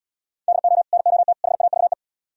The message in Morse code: DO NOT HAVE

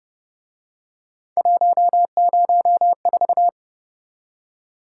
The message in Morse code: 104

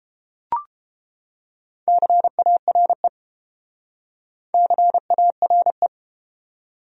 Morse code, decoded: CARE CARE

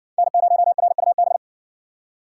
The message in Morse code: N0RUB

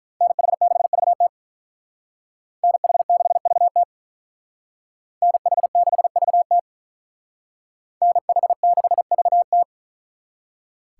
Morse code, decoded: NH6VT NH6VT NH6VT NH6VT

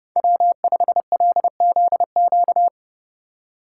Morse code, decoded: W5LZQ

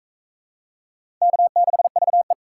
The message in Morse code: K6VE

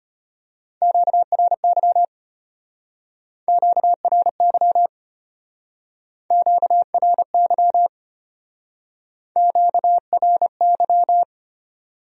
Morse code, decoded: QRY QRY QRY QRY

